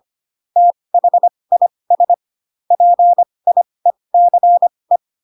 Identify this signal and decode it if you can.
Morse code, heard as a keyed tone, about 25 wpm, spaced as for 20 wpm, 700 Hz. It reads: THIS PIECE
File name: Pinocchio / Pinocchio ch1 15 25 ik1hge